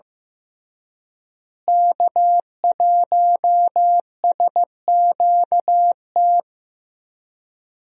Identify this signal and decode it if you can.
Morse code, heard as a keyed tone, about 15 wpm, 700 Hz. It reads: K1SQT
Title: USA Call Signs - 5 Characters - Repeated 3x - 15wpm